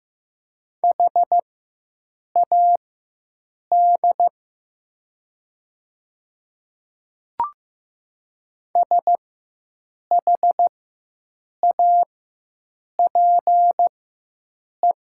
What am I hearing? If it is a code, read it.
Morse code, heard as a keyed tone, about 15 wpm, spaced as for 7 wpm, 700 Hz. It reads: HAD SHAPE
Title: 4x Character Spacing - Top 500 Words - 15wpm